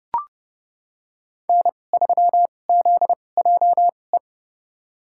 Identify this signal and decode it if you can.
Morse code, heard as a keyed tone, about 29 wpm, spaced as for 21 wpm, 700 Hz. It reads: N3ZJE
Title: USA Call Signs - 5 Characters - 2x Character Spacing - 30wpm